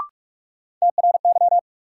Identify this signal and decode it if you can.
Morse code, heard as a keyed tone, about 41 wpm, 700 Hz. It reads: TRY
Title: Sets of 2 Words - New Word Drill - 45wpm